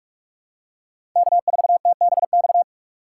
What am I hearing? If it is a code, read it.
Morse code, heard as a keyed tone, about 43 wpm, 700 Hz. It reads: K4TBX
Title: USA Call Signs - 5 Characters - 3x Character Spacing - 45wpm